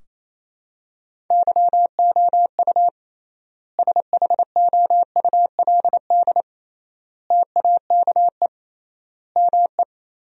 Morse code, heard as a keyed tone, about 28 wpm, 700 Hz. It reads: YOU SHOULD TAKE ME
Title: 3x Word-Spacing - Sentences from Top 300 Words - 28wpm